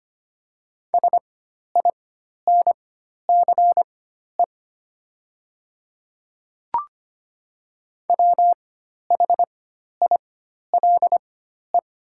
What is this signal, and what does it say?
Morse code, heard as a keyed tone, about 25 wpm, spaced as for 12 wpm, 700 Hz. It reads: SINCE WHILE
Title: 4x Character Spacing - Top 500 Words - 25wpm